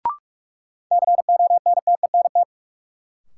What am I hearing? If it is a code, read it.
Morse code, heard as a keyed tone, about 42 wpm, 700 Hz. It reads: CONTENT
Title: Sentences from Top 900 Words - New Word Drill - 45wpm